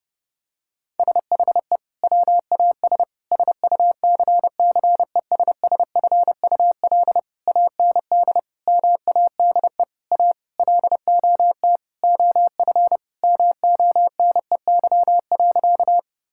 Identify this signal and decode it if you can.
Morse code, heard as a keyed tone, about 30 wpm, 700 Hz. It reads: SHE WAS SUCCESSFUL AND MADE A LOT OF MONEY.
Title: Sentences from Top 1000 Words - Review Part 6 - 30wpm